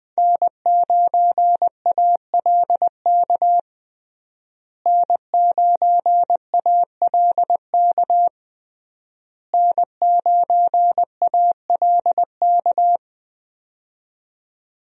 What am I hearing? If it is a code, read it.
Morse code, heard as a keyed tone, about 20 wpm, 700 Hz. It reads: N9ALK N9ALK N9ALK
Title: USA Call Signs - 5 Characters - Repeated 3x - 20wpm